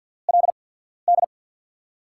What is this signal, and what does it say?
Morse code, heard as a keyed tone, about 48 wpm, spaced as for 14 wpm, 700 Hz. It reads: LD
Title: USA Call Signs - 6 Characters - 8x Character Spacing - 50wpm